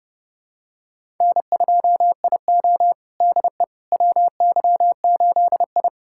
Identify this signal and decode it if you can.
Morse code, heard as a keyed tone, about 30 wpm, 700 Hz. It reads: N2IO DE WY8I
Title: USA Callsign DE Callsign - 4 Characters - 30wpm